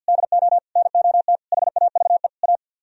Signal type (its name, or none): Morse code